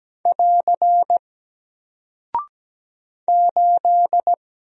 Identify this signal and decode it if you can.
Morse code, heard as a keyed tone, about 17 wpm, 700 Hz. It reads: <AR> 8
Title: Single Character (Letter:Number:Punctuation:Prosign) 17wpm